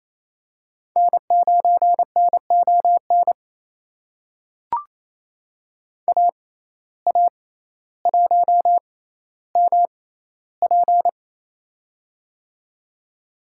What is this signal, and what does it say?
Morse code, heard as a keyed tone, about 28 wpm, spaced as for 14 wpm, 700 Hz. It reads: N9NON AA1MP
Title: USA Call Signs - 5 Characters - 6x Character Spacing - 28wpm